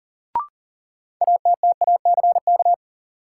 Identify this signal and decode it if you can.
Morse code, heard as a keyed tone, about 39 wpm, 700 Hz. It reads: ATTACK
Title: Sentences from Top 1500 Words - New Word Drill - 40wpm